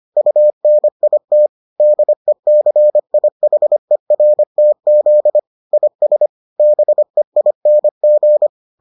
Morse code, heard as a keyed tone, about 25 wpm, 600 Hz. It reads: UNIT DECIHERTZ IS BEING